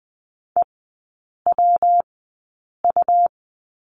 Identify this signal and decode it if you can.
Morse code, heard as a keyed tone, about 20 wpm, spaced as for 8 wpm, 700 Hz. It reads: EWU
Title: 114 Lesson 19e - Letter w - Three Characters 20wpm